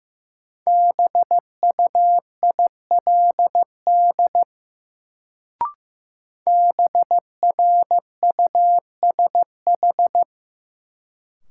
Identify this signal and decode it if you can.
Morse code, heard as a keyed tone, about 15 wpm, 700 Hz. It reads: BUILD BRUSH
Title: Sentences from Top 1100 Words - New Word Drill Alt - 15wpm